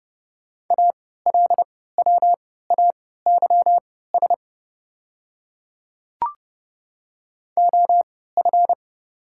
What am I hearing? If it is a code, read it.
Morse code, heard as a keyed tone, about 30 wpm, spaced as for 17 wpm, 690 Hz. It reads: ALWAYS OF
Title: 3x Character Spacing - Top 300 Words - 30wpm